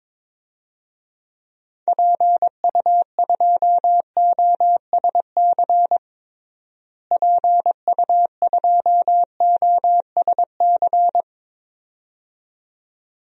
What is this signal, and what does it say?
Morse code, heard as a keyed tone, about 21 wpm, 700 Hz. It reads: PU2OSC PU2OSC
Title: DXCC-100 Call Signs - 6 Characters - Repeated 2x - 22wpm